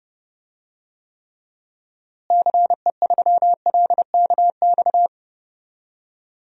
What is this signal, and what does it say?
Morse code, heard as a keyed tone, about 30 wpm, 700 Hz. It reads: CE3LKX
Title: DXCC-100 Call Signs - 6 Characters - Repeated 2x - 30wpm